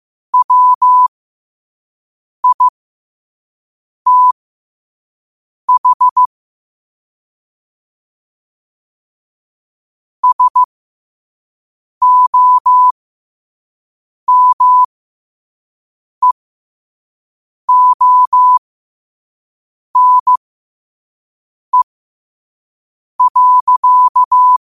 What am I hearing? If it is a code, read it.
Morse code, heard as a keyed tone, about 15 wpm, spaced as for 5 wpm, 1000 Hz. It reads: WITH SOMEONE.
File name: qotdfiles / 5wpm